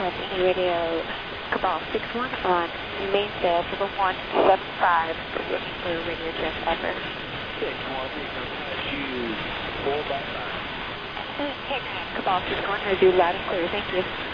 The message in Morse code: none

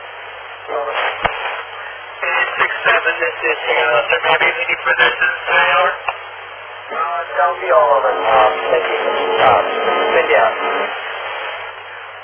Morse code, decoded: none

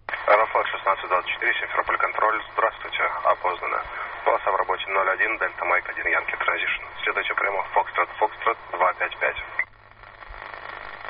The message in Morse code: none

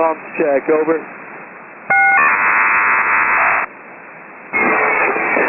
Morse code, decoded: none

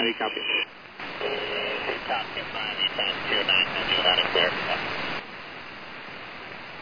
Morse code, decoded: none

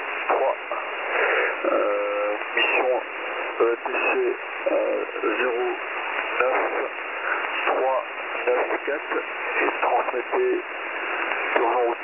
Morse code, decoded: none

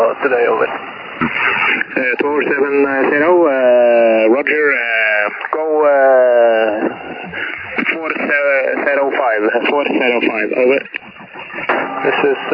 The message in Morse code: none

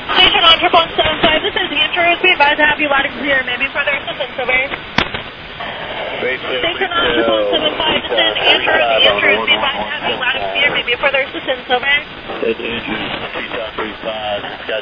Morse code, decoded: none

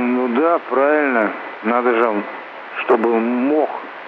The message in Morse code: none

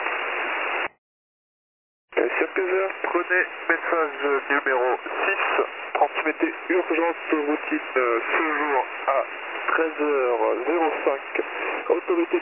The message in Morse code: none

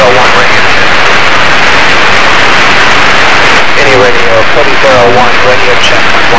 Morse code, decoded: none

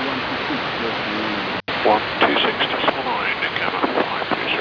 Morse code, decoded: none

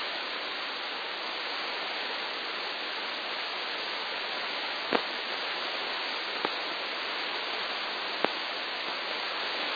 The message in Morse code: none

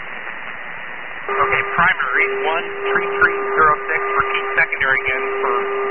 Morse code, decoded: none

